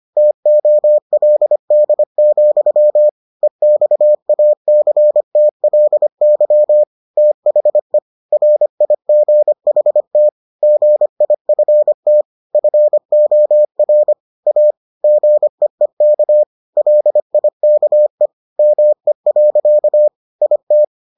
Morse code, heard as a keyed tone, about 25 wpm, 600 Hz. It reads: TOLD, EXACTLY THE RIGHT GIFT FOR A GEEK LIKE ME. IT